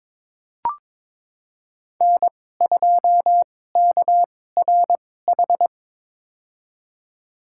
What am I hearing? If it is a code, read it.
Morse code, heard as a keyed tone, about 22 wpm, spaced as for 16 wpm, 700 Hz. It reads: N2KRH
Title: USA Call Signs - 5 Characters - 2x Character Spacing - 22wpm